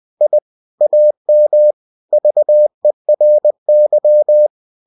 Morse code, heard as a keyed tone, about 20 wpm, 600 Hz. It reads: I AM VERY
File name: cw-books / winnie-the-pooh / winnie-20wpm-page-0101